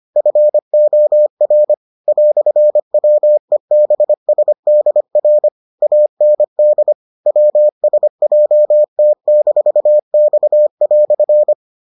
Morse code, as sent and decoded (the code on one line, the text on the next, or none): ..-. --- .-. / .-..-. .-- . -... ... -.. .-. / .- -. -.. / .-- ... .--- - -....- -..- .-..-.
FOR "WEBSDR AND WSJT-X"